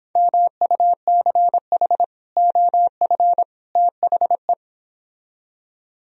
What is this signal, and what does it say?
Morse code, heard as a keyed tone, about 26 wpm, 700 Hz. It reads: MUCH OF THE
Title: Speed-Racing - Sets of 3 Words 22wpm